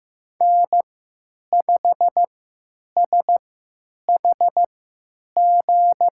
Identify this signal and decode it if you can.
Morse code, heard as a keyed tone, about 16 wpm, spaced as for 9 wpm, 700 Hz. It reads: N5SHG
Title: USA Call Signs - 3x Character Spacing - 15wpm